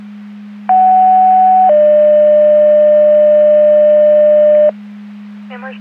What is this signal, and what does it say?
Two-tone page, heard as a one-second tone, then a three-second tone, about 750 then 600 Hz, with noise. It is heard from 0.7 to 4.7 s.
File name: Portage / 00045 14-51 Portage 154.385 Paging